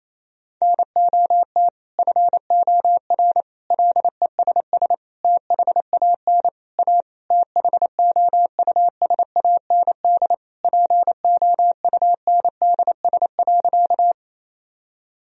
Morse code, heard as a keyed tone, about 28 wpm, 700 Hz. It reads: NOT FOR LESS THAN A THOUSAND POUNDS.
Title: Sentences from Top 1000 Words - Review Part 3 - 28wpm